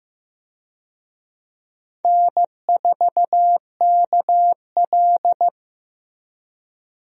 Morse code, heard as a keyed tone, about 15 wpm, 700 Hz. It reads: N4KL
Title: USA Call Signs - 4 Characters - Repeated 2x - 15wpm